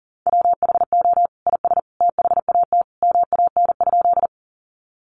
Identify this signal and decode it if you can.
Morse code, heard as a keyed tone, about 40 wpm, 700 Hz. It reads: WHO IS THAT MAN?
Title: Sentences from Top 200 Words 40wpm